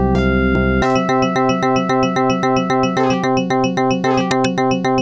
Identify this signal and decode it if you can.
There is no code.